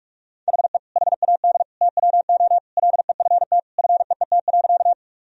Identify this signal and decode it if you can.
Morse code, heard as a keyed tone, about 46 wpm, 700 Hz. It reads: HE HAD TWO LEFT FEET.
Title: Sentences from Top 1000 Words - Review Part 1 - 45wpm